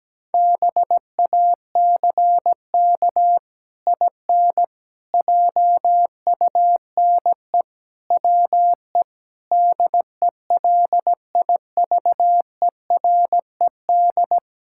Morse code, heard as a keyed tone, about 17 wpm, 700 Hz. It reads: BACK IN JUNE WE DELIVERED
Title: Pangram Challenge v2 - 17wpm